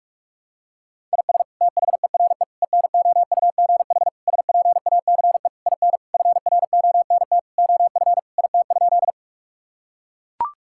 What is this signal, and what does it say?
Morse code, heard as a keyed tone, about 41 wpm, 700 Hz. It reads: IS THERE ENOUGH SPACE IN FRONT OF IT?
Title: Sentences from Top 1000 Words - Review Part 5 - 45wpm